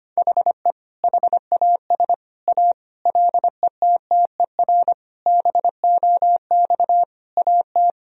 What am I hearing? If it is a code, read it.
Morse code, heard as a keyed tone, about 25 wpm, 700 Hz. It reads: HE HAS A LETTER BOX AT
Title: Sentences from Top 1000 Words - Review Part 2 - 25wpm